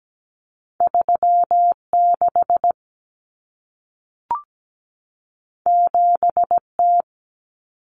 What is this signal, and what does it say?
Morse code, heard as a keyed tone, about 17 wpm, 700 Hz. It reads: 36 7T
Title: Common Cut Numbers - 2-Digit 17wpm